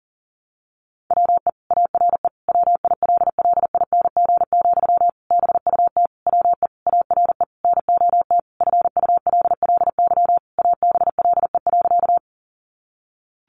Morse code, heard as a keyed tone, about 40 wpm, 700 Hz. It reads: WE ARE WILLING, BUT WE ARE NOT FULLY ABLE.
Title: Sentences from Top 1300 Words 40wpm